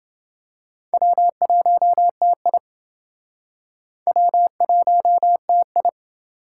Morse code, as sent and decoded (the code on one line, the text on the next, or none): .-- .---- - .. / .-- .---- - ..
W1TI W1TI